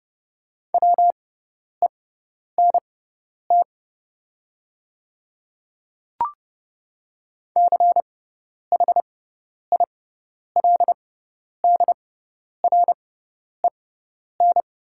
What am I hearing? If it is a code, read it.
Morse code, heard as a keyed tone, about 30 wpm, spaced as for 10 wpm, 700 Hz. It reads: WENT CHILDREN
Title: 6x Character Spacing - Top 500 Words - 30wpm